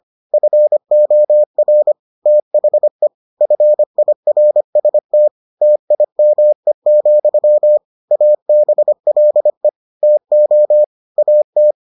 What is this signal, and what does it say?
Morse code, heard as a keyed tone, about 25 wpm, 600 Hz. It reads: FOR THE FIRST TIME, ABLE TO AT